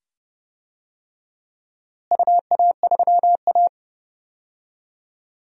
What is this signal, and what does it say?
Morse code, heard as a keyed tone, about 30 wpm, 700 Hz. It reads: UA3A
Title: DXCC-100 Call Signs - 4 Characters - Repeated 2x - 30wpm